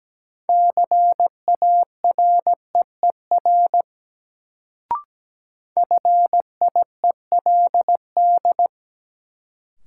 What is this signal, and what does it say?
Morse code, heard as a keyed tone, about 17 wpm, 700 Hz. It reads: CAREER FIELD
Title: Sentences from Top 1000 Words - New Word Drill - 17wpm